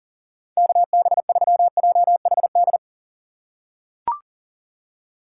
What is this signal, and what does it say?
Morse code, heard as a keyed tone, about 40 wpm, 700 Hz. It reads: KB3JHD
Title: USA Call Signs - 6 Characters - 3x Character Spacing - 40wpm